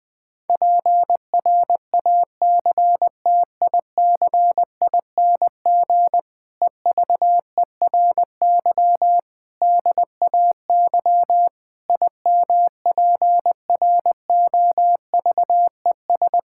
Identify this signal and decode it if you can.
Morse code, heard as a keyed tone, about 20 wpm, 700 Hz. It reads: PRACTICING EVERY DAY IMPROVES
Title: Difficult Sentences with Hints - 20wpm